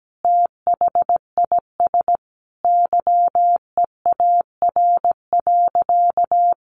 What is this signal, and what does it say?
Morse code, heard as a keyed tone, about 17 wpm, 700 Hz. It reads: THIS YEAR.
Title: Sentences from Top 1500 Words 17wpm